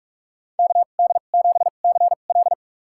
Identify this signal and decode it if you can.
Morse code, heard as a keyed tone, about 46 wpm, spaced as for 33 wpm, 700 Hz. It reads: KD7CL
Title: USA Call Signs - 5 Characters - 2x Character Spacing - 45wpm